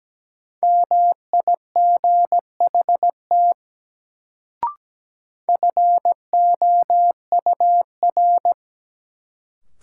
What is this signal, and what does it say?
Morse code, heard as a keyed tone, about 17 wpm, 700 Hz. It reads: MIGHT FOUR
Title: Sentences from Top 300 Words - New Word Drill - 17wpm